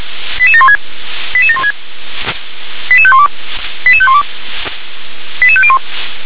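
Five-tone selective call (ZVEI), heard as quick runs of five tones, tones of about 70 ms. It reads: 88716 88716 88512 88512 88501
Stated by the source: ZVEI five-tone call 88631 received on an SDR.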